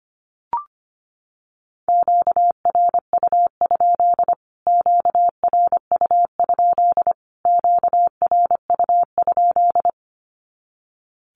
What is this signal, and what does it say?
Morse code, heard as a keyed tone, about 25 wpm, 700 Hz. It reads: QRU? QRU? QRU?
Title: Q-Codes Repeated 3x 25wpm